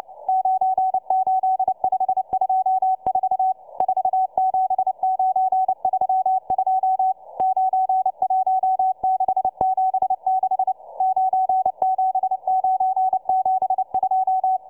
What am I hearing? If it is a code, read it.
Morse code, heard as a keyed tone, about 29 wpm, 750 Hz, with radio noise. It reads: 98524 47932 91676 97972